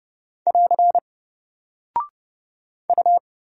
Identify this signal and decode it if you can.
Morse code, heard as a keyed tone, about 29 wpm, 700 Hz. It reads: <AR> U
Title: Single Character (Letter:Number:Punctuation:Prosign) 30wpm